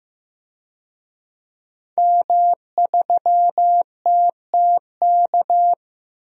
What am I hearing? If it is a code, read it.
Morse code, heard as a keyed tone, about 15 wpm, 700 Hz. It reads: M3TTK
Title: DXCC-100 Call Signs - 5 Characters - Repeated 2x - 15wpm